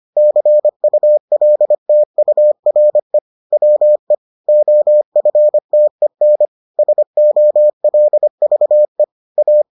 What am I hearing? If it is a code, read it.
Morse code, heard as a keyed tone, about 25 wpm, 600 Hz. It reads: CULTURE WE OFTEN SOLVE A